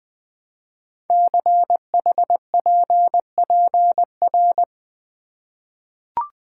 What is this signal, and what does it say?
Morse code, heard as a keyed tone, about 20 wpm, 700 Hz. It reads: CHPPR
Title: Code Groups - 5 Letters - 20wpm